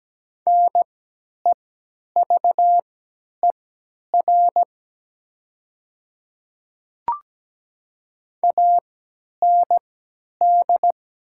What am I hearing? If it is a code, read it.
Morse code, heard as a keyed tone, about 17 wpm, spaced as for 10 wpm, 700 Hz. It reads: NEVER AND
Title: 3x Character Spacing - Top 500 Words - 17wpm